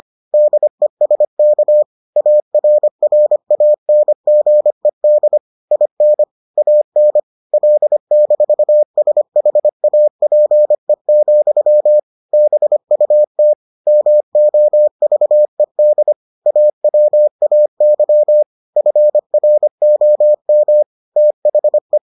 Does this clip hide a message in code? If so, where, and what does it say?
Yes: Morse code from 0.3 to 22.0 s, heard as a keyed tone, about 25 wpm, 600 Hz. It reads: DESK ARRANGED IN AN L-SHAPE, BUT MOVED AWAY FROM THE